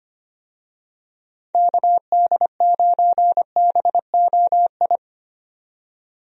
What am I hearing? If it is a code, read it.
Morse code, heard as a keyed tone, about 25 wpm, 700 Hz. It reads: KD9BOI
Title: USA Call Signs - 6 Characters - 25wpm